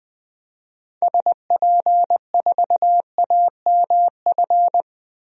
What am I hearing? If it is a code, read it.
Morse code, heard as a keyed tone, about 20 wpm, 700 Hz. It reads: SP4AMF
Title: DXCC-100 Call Signs - 6 Characters - Repeated 2x - 20wpm